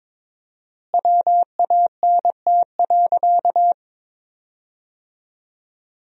Morse code, heard as a keyed tone, about 22 wpm, 700 Hz. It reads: WANT.
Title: 4x Word-Spacing - Sentences from Top 200 Words - 22wpm